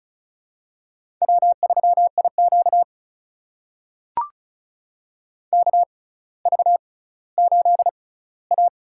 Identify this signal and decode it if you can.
Morse code, heard as a keyed tone, about 36 wpm, spaced as for 18 wpm, 700 Hz. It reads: W3IQ KV8A
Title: USA Call Signs - 4 Characters - 6x Character Spacing - 35wpm